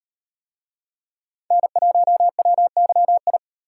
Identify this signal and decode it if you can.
Morse code, heard as a keyed tone, about 37 wpm, 700 Hz. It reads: N1WYI